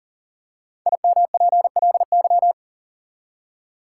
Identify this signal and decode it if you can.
Morse code, heard as a keyed tone, about 40 wpm, 700 Hz. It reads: IMPLY
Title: Five Letter Words - 40wpm